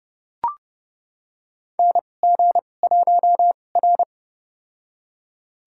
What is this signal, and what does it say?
Morse code, heard as a keyed tone, about 30 wpm, spaced as for 22 wpm, 700 Hz. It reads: NG1R
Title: USA Call Signs - 4 Characters - 2x Character Spacing - 30wpm